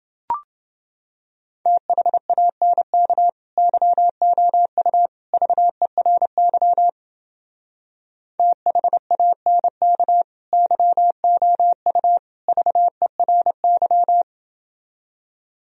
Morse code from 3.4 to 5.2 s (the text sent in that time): YOU